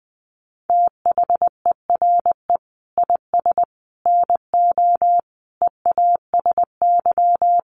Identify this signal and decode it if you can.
Morse code, heard as a keyed tone, about 20 wpm, 700 Hz. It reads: THERE IS NO EASY
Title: Sentences from Top 1300 Words 20wpm